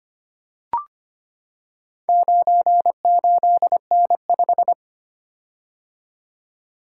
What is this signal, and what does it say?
Morse code, heard as a keyed tone, about 25 wpm, 700 Hz. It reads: 98N5
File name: practice / 25wpm / Four Letter-Number Combinations - 25wpm